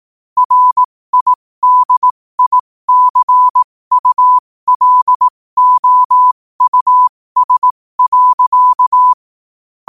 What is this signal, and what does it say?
Morse code, heard as a keyed tone, about 18 wpm, spaced as for 15 wpm, 1000 Hz. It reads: RIDICULOUS.
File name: qotdfiles / 15wpm